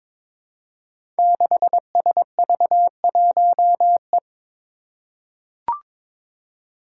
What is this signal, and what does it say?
Morse code, heard as a keyed tone, about 23 wpm, 700 Hz. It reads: 6SV1E
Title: Five Challenging Letter-Number Combinations 22wpm